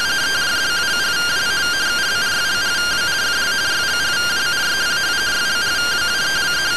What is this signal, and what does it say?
Two-tone teleprinter signal, about 170 Hz shift.